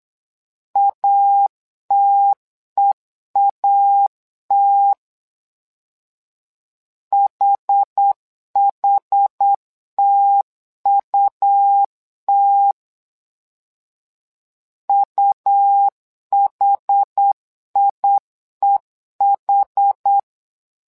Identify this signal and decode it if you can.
Morse code, heard as a keyed tone, about 8 wpm, 800 Hz. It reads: ATEAT HHTUT UHIEH
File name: LEZIONI / 2 UHDN / 2 UH 8wpm 2129